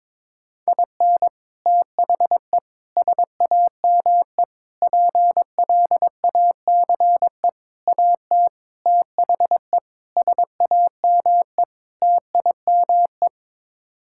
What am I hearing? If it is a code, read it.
Morse code, heard as a keyed tone, about 22 wpm, 700 Hz. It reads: IN THE SAME PLACE AT THE SAME TIME
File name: practice / 22wpm / Sets of 8 Words - 22wpm